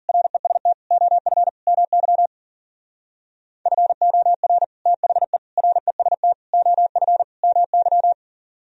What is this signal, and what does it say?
Morse code, heard as a keyed tone, about 49 wpm, 700 Hz. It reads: REST OF MY FOR THE REST OF MY
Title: Speed-Racing - Sets of 5 Words 40wpm